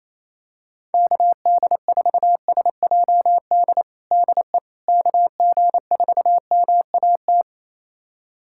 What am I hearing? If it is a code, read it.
Morse code, heard as a keyed tone, about 28 wpm, 700 Hz. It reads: KD4SJD DE KG4MAT
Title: USA Callsign DE Callsign 28wpm